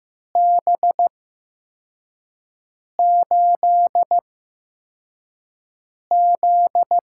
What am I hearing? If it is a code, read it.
Morse code, heard as a keyed tone, about 15 wpm, spaced as for 4 wpm, 700 Hz. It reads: B8Z
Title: USA Call Signs - 6 Characters - 8x Character Spacing - 15wpm